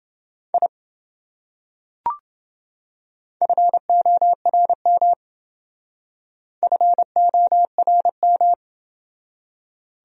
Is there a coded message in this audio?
Yes: Morse code from 0.5 to 8.5 s, heard as a keyed tone, about 25 wpm, 700 Hz. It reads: I FORM FORM